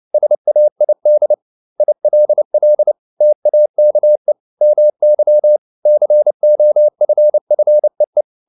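Morse code, heard as a keyed tone, about 29 wpm, 600 Hz. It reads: SAID ILL TAKE MY COFFEE
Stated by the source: Morse code at 29 wpm, sending Ken Follett's novel Never in English, from CW med Gnister.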